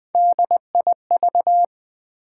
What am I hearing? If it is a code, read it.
Morse code, heard as a keyed tone, about 20 wpm, 700 Hz. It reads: DIV
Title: Build A Word - 3 Letters At A Time - 6-Letter Words - 20wpm